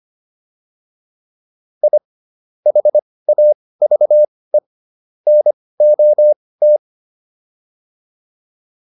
Morse code, heard as a keyed tone, about 25 wpm, spaced as for 18 wpm, 600 Hz. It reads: I HAVE NOT
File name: WD204 18